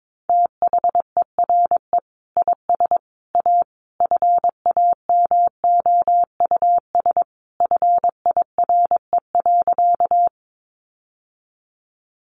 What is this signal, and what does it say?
Morse code, heard as a keyed tone, about 22 wpm, 700 Hz. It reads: THERE IS A FAMOUS FIRE.
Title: Sentences from Top 1500 Words 22wpm